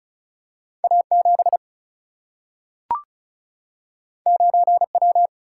Morse code, heard as a keyed tone, about 36 wpm, 700 Hz. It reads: A 7 9 W
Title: Single-Letter Single-Number 35wpm